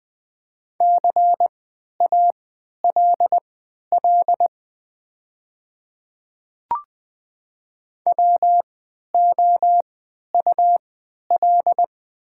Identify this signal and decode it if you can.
Morse code, heard as a keyed tone, about 20 wpm, spaced as for 11 wpm, 700 Hz. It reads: CALL WOUL
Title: 3x Character Spacing - Top 100 Words - 20wpm